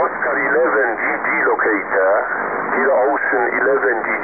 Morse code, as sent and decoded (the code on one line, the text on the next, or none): none